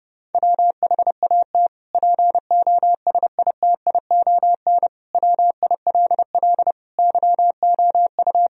Morse code, heard as a keyed tone, about 30 wpm, 700 Hz. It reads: WHAT POSITION WILL YOU
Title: Sentences from Top 1000 Words - Review Part 3 - 30wpm